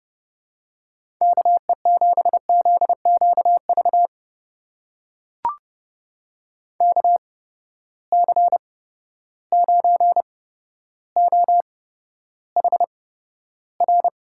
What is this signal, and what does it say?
Morse code, heard as a keyed tone, about 30 wpm, spaced as for 13 wpm, 700 Hz. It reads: KE7ZQV KC9OHR
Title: USA Call Signs - 6 Characters - 8x Character Spacing - 30wpm